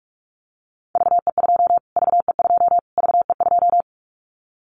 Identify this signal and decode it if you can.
Morse code, heard as a keyed tone, about 44 wpm, 700 Hz. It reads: VE2 VE2 VE2
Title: Callsign Prefixes for Top 25 DXCC - Repeated 3x - 45wpm